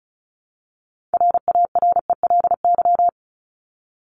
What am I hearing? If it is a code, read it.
Morse code, heard as a keyed tone, about 34 wpm, 700 Hz. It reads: RARELY
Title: LY Suffix - Top 100 Words with Base Words 35wpm